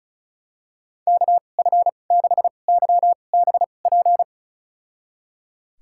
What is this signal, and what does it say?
Morse code, heard as a keyed tone, about 34 wpm, spaced as for 25 wpm, 690 Hz. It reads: KF6YBP
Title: USA Call Signs - 6 Characters - 2x Character Spacing - 35wpm